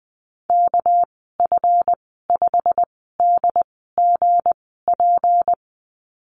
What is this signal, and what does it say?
Morse code, heard as a keyed tone, about 20 wpm, spaced as for 14 wpm, 700 Hz. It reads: KF5DGP
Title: 238 Lesson 38g - Letter q - Call Signs 20wpm